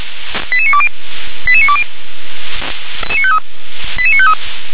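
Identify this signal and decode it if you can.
Five-tone selective call (ZVEI), heard as quick runs of five tones, tones of about 70 ms. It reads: 88920 88920 88953
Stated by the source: Five-tone ZVEI selective call 88955 received by SDR.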